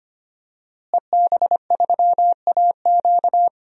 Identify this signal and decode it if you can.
Morse code, heard as a keyed tone, about 25 wpm, 700 Hz. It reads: EB3AQ
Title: DXCC-100 Call Signs - 5 Characters - Repeated 2x - 25wpm